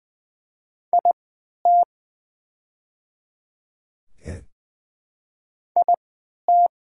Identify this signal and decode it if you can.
Morse code, heard as a keyed tone, about 20 wpm, spaced as for 11 wpm, 700 Hz. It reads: IT IT
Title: 3x Character Spacing - Top 500 Words - 20wpm